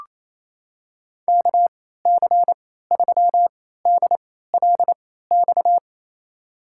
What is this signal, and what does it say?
Morse code, heard as a keyed tone, about 28 wpm, spaced as for 16 wpm, 700 Hz. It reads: KC3DLX
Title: USA Call Signs - 6 Characters - 3x Character Spacing - 28wpm